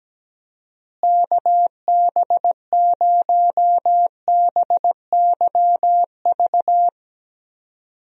Morse code, heard as a keyed tone, about 17 wpm, 700 Hz. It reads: KB0BYV
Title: USA Call Signs - 6 Characters - Repeated 3x - 17wpm